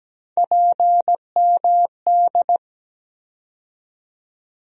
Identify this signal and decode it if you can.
Morse code, heard as a keyed tone, about 17 wpm, 700 Hz. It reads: PMD
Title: USA Call Sign Postfixes - 3 Letters 17wpm